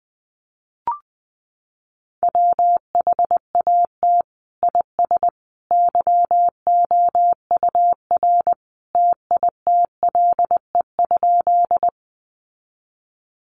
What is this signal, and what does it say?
Morse code, heard as a keyed tone, about 20 wpm, 700 Hz. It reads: WHAT IS YOUR TITLE?
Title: Sentences from Top 1200 Words 20wpm